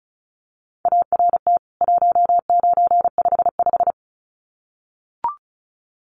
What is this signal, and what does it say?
Morse code, heard as a keyed tone, about 34 wpm, 700 Hz. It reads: ART 1955
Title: CWT Exchange 35wpm